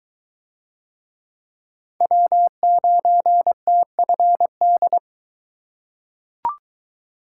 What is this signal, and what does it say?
Morse code, heard as a keyed tone, about 21 wpm, 700 Hz. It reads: W9TFD